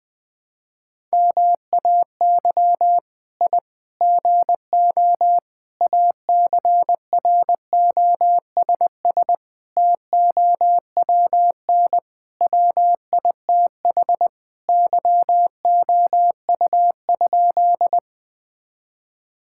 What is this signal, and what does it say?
Morse code, heard as a keyed tone, about 20 wpm, 700 Hz. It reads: MAY I GO ACROSS TOWN WITH YOU?
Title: Sentences from Top 1000 Words - Review Part 5 - 20wpm